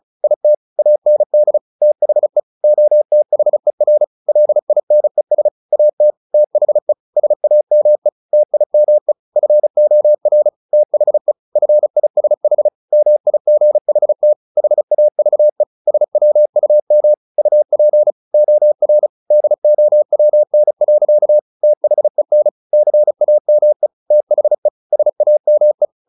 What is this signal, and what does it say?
Morse code, heard as a keyed tone, about 34 wpm, 600 Hz. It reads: IT AND THE OTHER LINES AT THE SAME TIME FOR THE FISH MIGHT HAVE SWUM UP OR DOWN. THEN CAME THE SAME
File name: the old man and the sea / the old man and the sea 1 35wpm